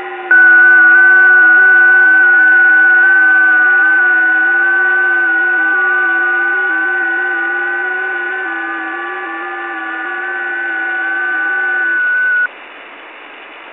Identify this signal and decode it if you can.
FT8.
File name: test / websdr test19